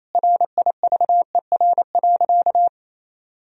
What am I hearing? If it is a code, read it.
Morse code, heard as a keyed tone, about 28 wpm, 700 Hz. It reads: RIVER.
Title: Sentences from Top 1000 Words - Review Part 4 - 28wpm